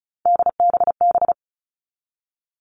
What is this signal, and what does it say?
Morse code, heard as a keyed tone, about 34 wpm, 700 Hz. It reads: DBB
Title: All Cut Numbers - 3-Digit 35wpm